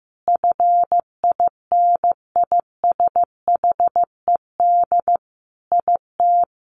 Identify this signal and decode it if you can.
Morse code, heard as a keyed tone, about 15 wpm, 700 Hz. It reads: FINISHED IT
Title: A Sound Of Thunder 15wpm